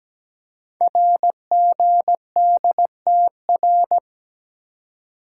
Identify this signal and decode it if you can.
Morse code, heard as a keyed tone, about 17 wpm, 700 Hz. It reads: RGDTR
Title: Code Groups - 5 Letters - 17wpm